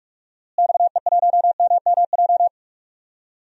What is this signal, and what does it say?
Morse code, heard as a keyed tone, about 42 wpm, 700 Hz. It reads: XE1MMJ
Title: DXCC-100 Call Signs - 6 Characters - Repeated 2x - 45wpm